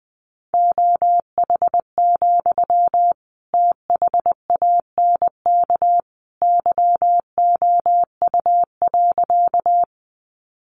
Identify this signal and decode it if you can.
Morse code, heard as a keyed tone, about 20 wpm, 700 Hz. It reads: OH, THANK YOU.
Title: Sentences from Top 600 Words 20wpm